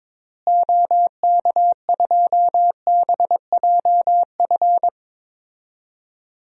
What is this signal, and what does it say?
Morse code, heard as a keyed tone, about 22 wpm, 700 Hz. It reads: OK2BJF
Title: DXCC-100 Call Signs - 6 Characters - Repeated 3x - 22wpm